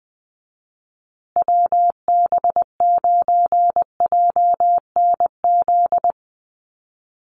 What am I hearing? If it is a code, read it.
Morse code, heard as a keyed tone, about 19 wpm, 700 Hz. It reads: WB9JNZ
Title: USA Call Signs Repeated 2x 20wpm